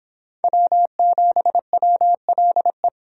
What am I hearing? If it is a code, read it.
Morse code, heard as a keyed tone, about 26 wpm, 700 Hz. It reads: W7WLE